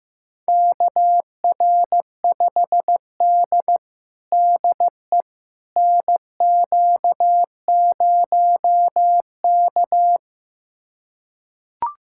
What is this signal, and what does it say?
Morse code, heard as a keyed tone, about 16 wpm, 700 Hz. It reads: KR5D DE NQ0K
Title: USA Callsign DE Callsign - 4 Characters - 15wpm